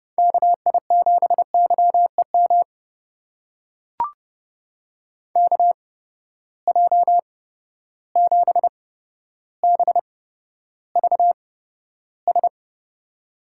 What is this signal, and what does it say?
Morse code, heard as a keyed tone, about 30 wpm, spaced as for 13 wpm, 700 Hz. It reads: KI7YEM KJ7BVS
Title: USA Call Signs - 6 Characters - 8x Character Spacing - 30wpm